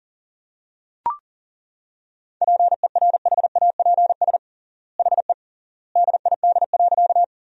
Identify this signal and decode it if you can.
Morse code, heard as a keyed tone, about 37 wpm, 700 Hz. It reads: PERHAPS HE DID.
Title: 3x Word-Spacing - Sentences from Top 500 Words - 40wpm